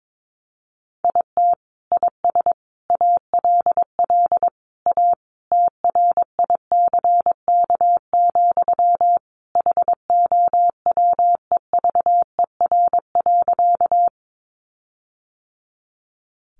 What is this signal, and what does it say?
Morse code, heard as a keyed tone, about 22 wpm, 700 Hz. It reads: IT IS ALL A TRICK, HOWEVER.
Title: Sentences from Top 1500 Words 22wpm